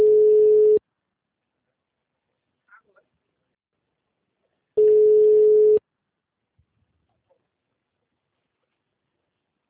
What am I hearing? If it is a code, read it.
Ringback tone (Europe, 425 Hz).
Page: Rio 3 - Ura